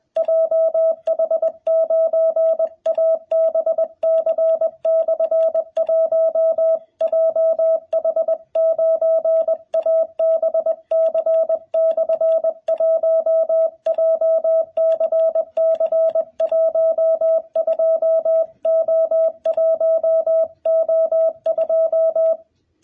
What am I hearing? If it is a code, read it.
Morse code, heard as a keyed tone, about 21 wpm, 660 Hz, with radio noise. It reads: JH9ABC/1 JH9ABC/1 JCC12O1O2
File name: jh8cbh / 2021cw